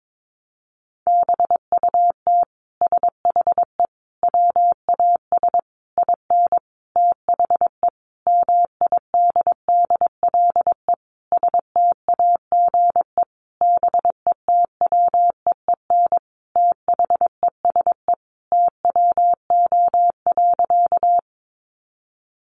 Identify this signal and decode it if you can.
Morse code, heard as a keyed tone, about 22 wpm, 700 Hz. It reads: BUT SHE WAS IN THE MIDDLE STAGE BETWEEN THESE TWO.
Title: Sentences from Top 1200 Words 22wpm